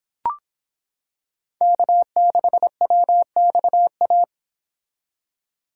Morse code, heard as a keyed tone, about 26 wpm, 700 Hz. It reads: K6WXA